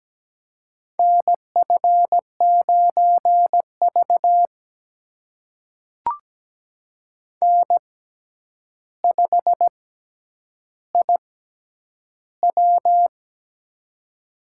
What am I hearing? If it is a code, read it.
Morse code, heard as a keyed tone, about 17 wpm, spaced as for 9 wpm, 700 Hz. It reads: NF9V N5IW
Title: USA Call Signs - 4 Characters - 6x Character Spacing - 17wpm